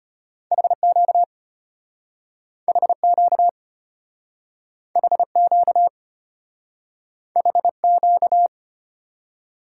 Morse code, heard as a keyed tone, about 34 wpm, 700 Hz. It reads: HQ HQ HQ HQ